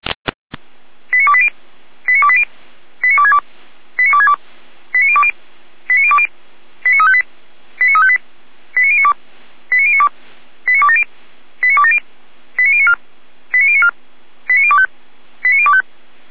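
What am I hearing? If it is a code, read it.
Five-tone selective call (ZVEI), heard as quick runs of five tones, tones of about 70 ms. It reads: 89280 89280 89262 89262 89020 89020 89358 89358 89092 89092 89270 89270 89094 89094 89025 89025
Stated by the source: Five-tone ZVEI selective call 89330 received on an SDR, recorded by Feuerwehr Orlamünde.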